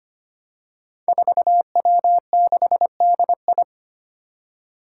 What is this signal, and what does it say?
Morse code, heard as a keyed tone, about 25 wpm, 700 Hz. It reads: 4W6DI
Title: Five Challenging Letter-Number Combinations 25wpm